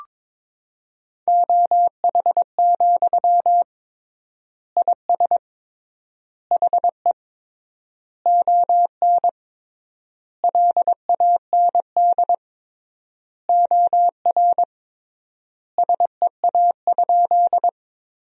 Morse code, heard as a keyed tone, about 22 wpm, 700 Hz. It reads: OH, IS HE ON LAND OR SEA?
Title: 3x Word-Spacing - Sentences from Top 500 Words - 22wpm